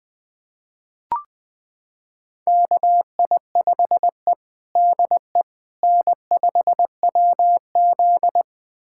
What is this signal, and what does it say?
Morse code, heard as a keyed tone, about 20 wpm, 700 Hz. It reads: KI5E DE N5WZ
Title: USA Callsign DE Callsign - 4 Characters - 20wpm